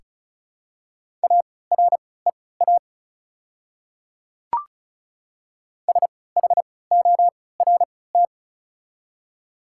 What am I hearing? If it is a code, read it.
Morse code, heard as a keyed tone, about 34 wpm, spaced as for 20 wpm, 700 Hz. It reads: AREA SHORT
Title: 3x Character Spacing - Top 500 Words - 35wpm